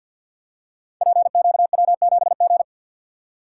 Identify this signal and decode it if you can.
Morse code, heard as a keyed tone, about 53 wpm, 700 Hz. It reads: PQW7G
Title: Code Groups - 5 Letter:Numbers - 50wpm